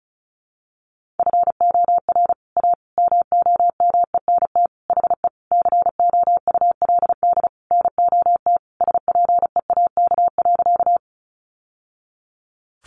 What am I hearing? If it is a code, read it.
Morse code, heard as a keyed tone, about 34 wpm, 700 Hz. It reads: FOR A MOMENT HE COULD NOT SPEAK.
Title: Sentences from Top 1100 Words 35wpm